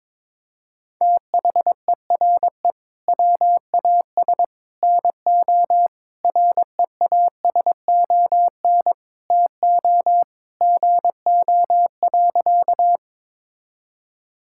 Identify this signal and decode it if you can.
Morse code, heard as a keyed tone, about 22 wpm, 700 Hz. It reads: THERE WAS NO REASON TO GO.
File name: practice / 22wpm / Sentences with 3-Word Phrases - 22wpm